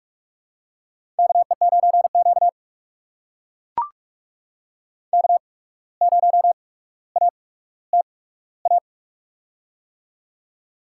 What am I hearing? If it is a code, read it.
Morse code, heard as a keyed tone, about 44 wpm, spaced as for 18 wpm, 700 Hz. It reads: KE9Q K0ATA
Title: USA Call Signs - 8x Character Spacing - 45wpm